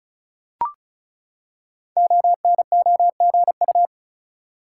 Morse code, heard as a keyed tone, about 36 wpm, 700 Hz. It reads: ONOGU
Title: Code Groups - 5 Letters - 35wpm